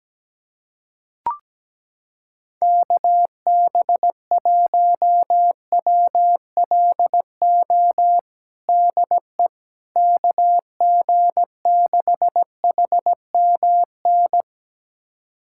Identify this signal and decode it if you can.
Morse code, heard as a keyed tone, about 17 wpm, 700 Hz. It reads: KB1WLO DE KG6HMN
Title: USA Callsign DE Callsign 17wpm